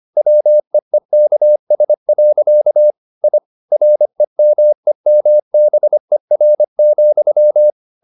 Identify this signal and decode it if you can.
Morse code, heard as a keyed tone, about 25 wpm, 600 Hz. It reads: WEEKS. I REMEMBER,
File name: the old man and the sea part 1 25wpm